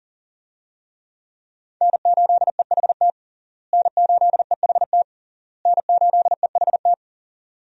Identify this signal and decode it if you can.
Morse code, heard as a keyed tone, about 40 wpm, 700 Hz. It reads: N8EHT N8EHT N8EHT
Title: USA Call Signs - 5 Characters - Repeated 3x - 40wpm